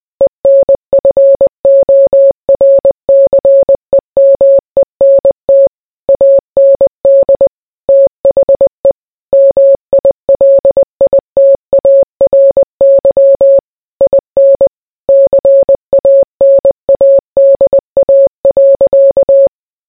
Morse code, heard as a keyed tone, about 20 wpm, 550 Hz. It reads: ENFORCEMENT AND THE MILITARY IN CANADA.